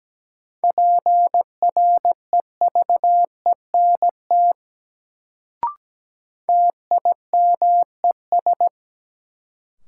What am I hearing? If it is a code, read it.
Morse code, heard as a keyed tone, about 17 wpm, 700 Hz. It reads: PREVENT TIMES
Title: Sentences from Top 1100 Words - New Word Drill Alt - 17wpm